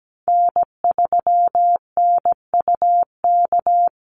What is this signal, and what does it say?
Morse code, heard as a keyed tone, about 17 wpm, 700 Hz. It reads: N3NUK
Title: USA Call Signs Repeated 2x 17wpm